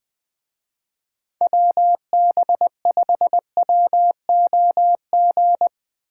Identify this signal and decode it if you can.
Morse code, heard as a keyed tone, about 22 wpm, 700 Hz. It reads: WB5WOG
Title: Speed Racing - USA 2x3 Call Signs 20wpm